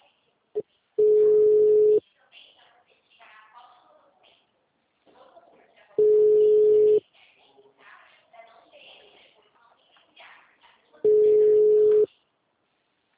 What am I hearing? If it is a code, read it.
Ringback tone (Europe, 425 Hz), with speech.